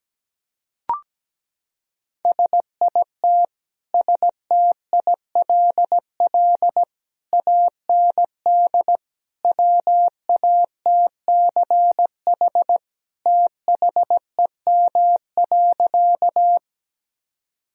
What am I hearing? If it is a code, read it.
Morse code, heard as a keyed tone, about 17 wpm, 700 Hz. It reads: SIT STILL AND WATCH THEM.
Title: Sentences from Top 1000 Words - Review Part 1 - 17wpm